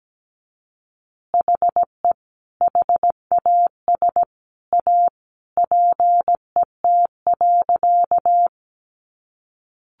Morse code, heard as a keyed tone, about 17 wpm, 700 Hz. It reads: HE HAS A PET.
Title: Easy Phrases 17wpm